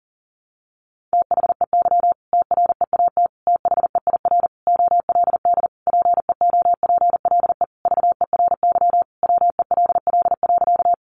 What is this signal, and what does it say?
Morse code, heard as a keyed tone, about 40 wpm, 700 Hz. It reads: THEY TREAT THEIR OLD PEOPLE VERY WELL.
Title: Sentences from Top 1300 Words 40wpm